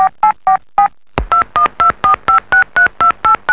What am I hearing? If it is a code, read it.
DTMF, heard as telephone touch tones, tones of about 100 ms. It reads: 8 2124563289
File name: lin / 声音 / 电话拨号